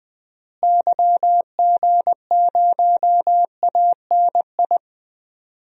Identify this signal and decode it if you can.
Morse code, heard as a keyed tone, about 21 wpm, 700 Hz. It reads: YG0ANI
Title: DXCC-100 Call Signs - 6 Characters - Repeated 3x - 20wpm